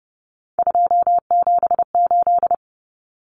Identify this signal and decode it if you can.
Morse code, heard as a keyed tone, about 30 wpm, 700 Hz. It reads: 278
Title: Common Cut Numbers - 3-Digit 30wpm